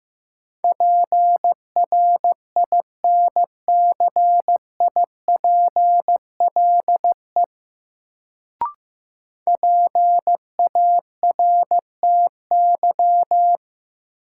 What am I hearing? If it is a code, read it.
Morse code, heard as a keyed tone, about 16 wpm, 700 Hz. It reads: PRINCIPLE PARTY
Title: Sentences from Top 1400 Words - New Word Drill - 15wpm